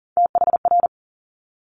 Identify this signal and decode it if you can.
Morse code, heard as a keyed tone, about 39 wpm, 700 Hz. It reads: THR
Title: Multi-letter Phonemes 40wpm